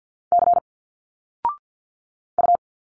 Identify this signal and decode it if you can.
Morse code, heard as a keyed tone, about 50 wpm, 700 Hz. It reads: C U
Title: Single Letters 50wpm